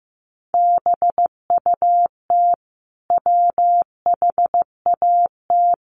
Morse code, heard as a keyed tone, about 14 wpm, 700 Hz. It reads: BUT WHAT
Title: Sentences from Top 100 Words 15wpm